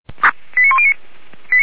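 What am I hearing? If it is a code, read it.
Five-tone selective call (ZVEI), heard as quick runs of five tones, tones of about 70 ms. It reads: 89109
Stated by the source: ZVEI five-tone selective call received on SDR.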